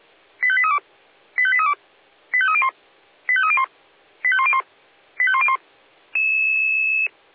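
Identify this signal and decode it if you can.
Five-tone selective call (ZVEI), heard as quick runs of five tones, tones of about 70 ms. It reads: 85832 85832 85301 85301 85101 85101
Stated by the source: A ZVEI five-tone selective call received by SDR.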